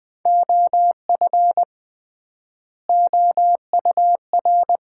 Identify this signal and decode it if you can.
Morse code, heard as a keyed tone, about 20 wpm, 700 Hz. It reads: OF OUR
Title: 3x Word-Spacing - Sets of 2 Words - 20wpm